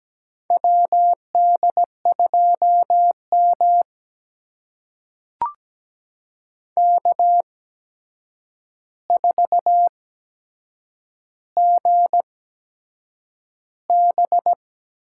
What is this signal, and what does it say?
Morse code, heard as a keyed tone, about 17 wpm, spaced as for 7 wpm, 700 Hz. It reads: WD2M K4GB
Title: USA Call Signs - 4 Characters - 8x Character Spacing - 17wpm